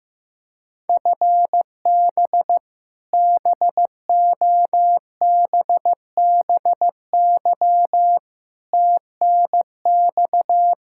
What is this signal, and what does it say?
Morse code, heard as a keyed tone, about 15 wpm, 700 Hz. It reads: FB BOBBY TNX
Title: Practice QSO - Exchange 3 - More Info - 15wpm